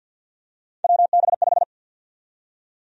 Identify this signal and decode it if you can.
Morse code, heard as a keyed tone, about 53 wpm, 700 Hz. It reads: WB5
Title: USA Call Sign Prefixes - 50wpm